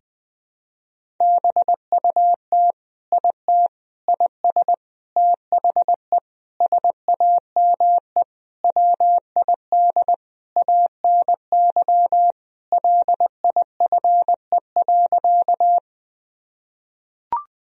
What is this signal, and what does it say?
Morse code, heard as a keyed tone, about 20 wpm, 700 Hz. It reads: BUT IT IS THE SAME WID ANY LIFE.
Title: Sentences from Top 300 Words and Abbreviations 20wpm